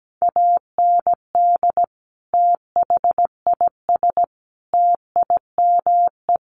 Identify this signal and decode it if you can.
Morse code, heard as a keyed tone, about 17 wpm, 700 Hz. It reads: AND THIS TIME
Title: Sentences from Top 1400 Words 17wpm